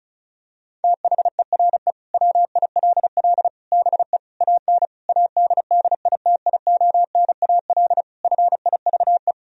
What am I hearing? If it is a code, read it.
Morse code, heard as a keyed tone, about 36 wpm, 700 Hz. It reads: THERE WILL BE AN ADDITIONAL FIVE
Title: Sentences from Top 1000 Words - Review Part 4 - 35wpm